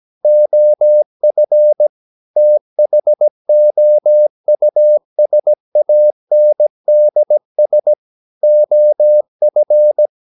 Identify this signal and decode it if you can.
Morse code, heard as a keyed tone, about 17 wpm, 600 Hz. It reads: OF THOUSANDS OF
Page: Romanen Never af Ken Follet på engelsk.